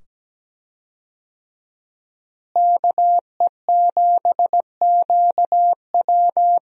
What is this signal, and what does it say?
Morse code, heard as a keyed tone, about 17 wpm, 700 Hz. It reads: KE7QW
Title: USA Call Signs - 5 Characters - Repeated 2x - 17wpm